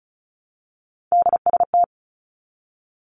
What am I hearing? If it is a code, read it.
Morse code, heard as a keyed tone, about 36 wpm, 700 Hz. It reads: DST
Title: Three Challenging Letter-Number Combinations 35wpm